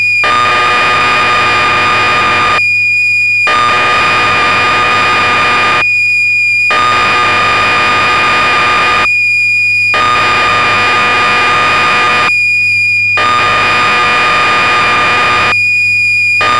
Cassette data (Kansas City type).